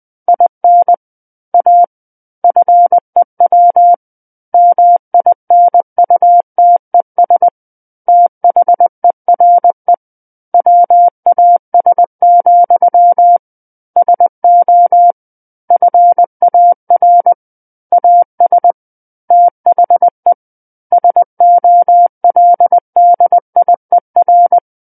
Morse code, heard as a keyed tone, about 20 wpm, 700 Hz. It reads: IN A FEW MINUTES THERE WAS, SO FAR AS THE SOLDIER
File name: 20 WPM